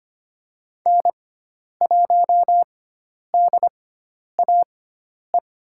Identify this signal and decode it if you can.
Morse code, heard as a keyed tone, about 25 wpm, spaced as for 10 wpm, 700 Hz. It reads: N1DAE
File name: Lesson 16g - Letter c - Call Signs 25wpm